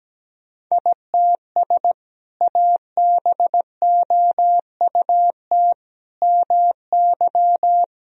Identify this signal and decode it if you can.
Morse code, heard as a keyed tone, about 17 wpm, 700 Hz. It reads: ITS ABOUT MY
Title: Sentences from Top 1000 Words - Review Part 1 - 17wpm